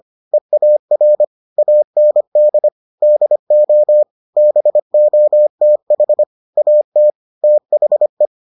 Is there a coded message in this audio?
Yes: Morse code from 0.3 to 8.3 s, heard as a keyed tone, about 25 wpm, 600 Hz. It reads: EAR AND DO BOTH AT THE